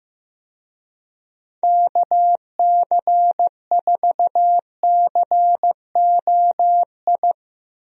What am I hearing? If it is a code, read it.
Morse code, heard as a keyed tone, about 15 wpm, 700 Hz. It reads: KC4COI
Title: USA Call Signs - 6 Characters - Repeated 2x - 15wpm